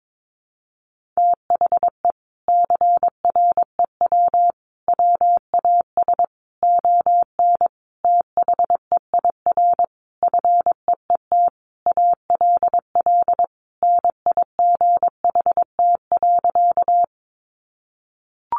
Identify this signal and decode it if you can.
Morse code, heard as a keyed tone, about 22 wpm, 700 Hz. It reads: THE CREW WAS ON THEIR FEET ALL NIGHT.
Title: Sentences from Top 1500 Words 22wpm